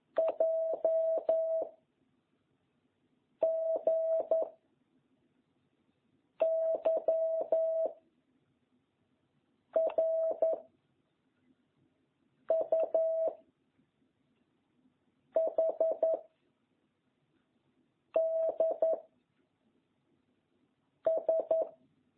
Morse code, heard as a keyed tone, about 11 wpm, 660 Hz. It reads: J G Y R U H D S